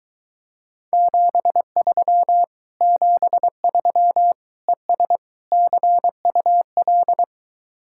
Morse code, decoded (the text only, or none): 73 73 ES CUL